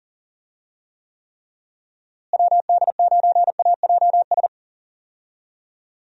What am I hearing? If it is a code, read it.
Morse code, heard as a keyed tone, about 37 wpm, 700 Hz. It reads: WD9AJS
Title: USA Call Signs - 6 Characters - Repeated 3x - 40wpm